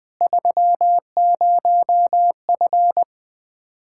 Morse code, heard as a keyed tone, about 20 wpm, 700 Hz. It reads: 30F
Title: K1USN SST QSO Practice 20wpm